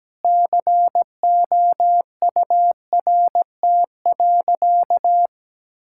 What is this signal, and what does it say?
Morse code, heard as a keyed tone, about 17 wpm, 700 Hz. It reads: COURT.
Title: Sentences from Top 1600 Words 17wpm